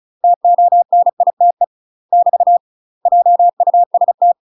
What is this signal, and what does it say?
Morse code, heard as a keyed tone, about 36 wpm, 700 Hz. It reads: TONITE <BT> JUST
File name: UK MASTER QSOs TRAINING / 35WPM / UK MASTER QSOs 0022